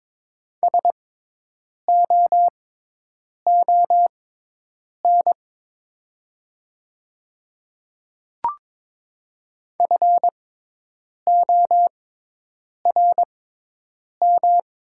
Morse code, heard as a keyed tone, about 22 wpm, spaced as for 8 wpm, 700 Hz. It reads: SOON FORM
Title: 6x Character Spacing - Top 500 Words - 22wpm